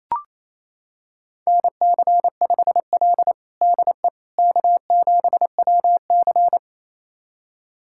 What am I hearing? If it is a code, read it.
Morse code, heard as a keyed tone, about 27 wpm, 700 Hz. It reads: NC5L DE K7WC
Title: USA Callsign DE Callsign - 4 Characters - 28wpm